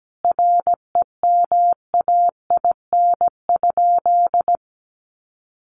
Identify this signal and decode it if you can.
Morse code, heard as a keyed tone, about 17 wpm, 700 Hz. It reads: REMAIN?
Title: Sentences from Top 1200 Words 17wpm